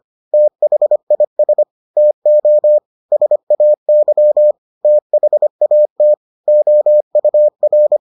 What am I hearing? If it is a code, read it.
Morse code, heard as a keyed tone, about 25 wpm, 600 Hz. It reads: THIS TO SAY THAT OUR